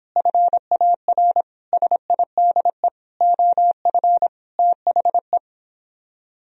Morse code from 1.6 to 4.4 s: SIDE OF

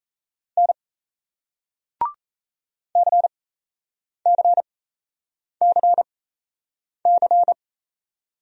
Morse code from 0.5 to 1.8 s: N

